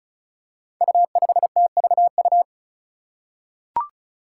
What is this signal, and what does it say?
Morse code, heard as a keyed tone, about 34 wpm, 700 Hz. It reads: U5TVU
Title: Five Challenging Letter-Number Combinations 35wpm